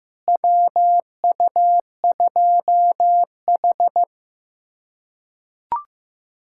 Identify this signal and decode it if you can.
Morse code, heard as a keyed tone, about 15 wpm, 700 Hz. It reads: WU2H
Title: USA Callsign DE Callsign - 4 Characters - 15wpm